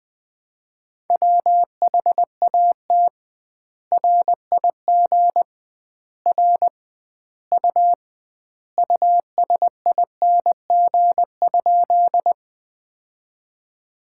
Morse code, heard as a keyed tone, about 19 wpm, 700 Hz. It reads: WHAT RIG R U USING?
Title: Common QSO Phrases - 2x Word-Spacing - 20wpm